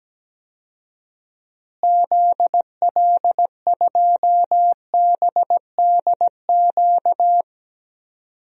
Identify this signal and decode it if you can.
Morse code, heard as a keyed tone, about 17 wpm, 700 Hz. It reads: ZL2BDQ
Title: DXCC-100 Call Signs - 6 Characters - Repeated 2x - 17wpm